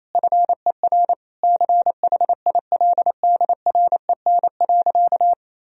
Morse code, heard as a keyed tone, about 28 wpm, 700 Hz. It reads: FER CHILDREN.
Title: Sentences from Top 400 Words and Abbreviations 28wpm